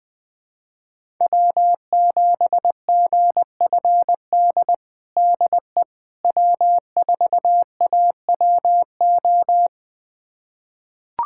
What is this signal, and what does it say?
Morse code, heard as a keyed tone, about 20 wpm, 700 Hz. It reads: W7GFD DE W4AWO
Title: USA Callsign DE Callsign - 5 Characters - 20wpm